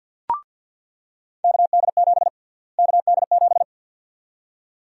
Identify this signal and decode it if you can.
Morse code, heard as a keyed tone, about 49 wpm, 700 Hz. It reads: KD7 KD7
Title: USA Call Sign Prefixes - Repeated 2x - 50wpm